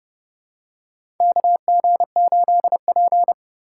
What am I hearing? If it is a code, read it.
Morse code, heard as a keyed tone, about 31 wpm, 700 Hz. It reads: KG8P
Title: USA Call Signs - 3x Character Spacing - 30wpm